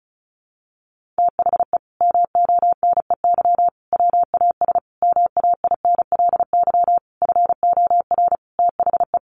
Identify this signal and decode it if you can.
Morse code, heard as a keyed tone, about 34 wpm, 700 Hz. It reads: THE MONEY WAS MAINLY FOR THE
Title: Sentences from Top 1400 Words 35wpm